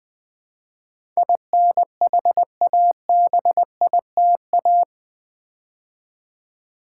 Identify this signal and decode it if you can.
Morse code, heard as a keyed tone, about 20 wpm, 700 Hz. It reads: INHABITA
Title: Build A Word - 4 Letters At A Time - 10-Letter Words - 20wpm